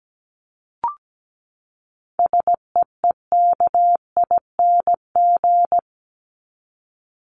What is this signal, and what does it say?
Morse code, heard as a keyed tone, about 17 wpm, 700 Hz. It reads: SEEKING
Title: ING Suffix - Top 100 Words with Base Words 17wpm